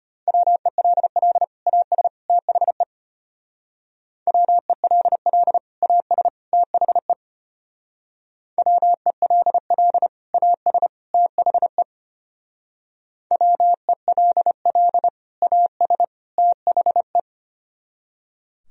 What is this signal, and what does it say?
Morse code, heard as a keyed tone, about 31 wpm, 700 Hz. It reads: WELL AS THE WELL AS THE WELL AS THE WELL AS THE